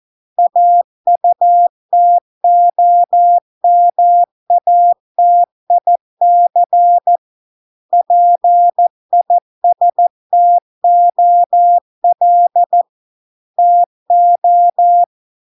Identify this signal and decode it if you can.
Morse code, heard as a keyed tone, about 14 wpm, 700 Hz. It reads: AUTOMATIC PISTOL TO